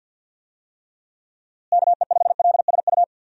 Morse code, heard as a keyed tone, about 48 wpm, 700 Hz. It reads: KE5LSU
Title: USA Call Signs - 6 Characters - Repeated 2x - 50wpm